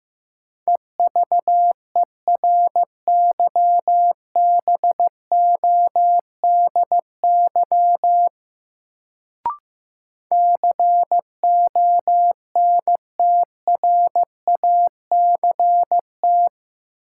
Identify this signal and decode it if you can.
Morse code, heard as a keyed tone, about 15 wpm, 700 Hz. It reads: EVERYBODY CONTRACT
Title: Sentences from Top 1400 Words - New Word Drill - 15wpm